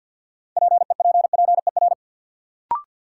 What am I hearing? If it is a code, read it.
Morse code, heard as a keyed tone, about 53 wpm, 700 Hz. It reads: PEPPER
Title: Sets of 5 Words - New Word Drill - 50wpm